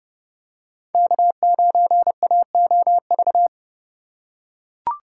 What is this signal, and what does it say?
Morse code, heard as a keyed tone, about 32 wpm, 700 Hz. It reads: K9AOV
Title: USA Call Signs - 5 Characters - 3x Character Spacing - 30wpm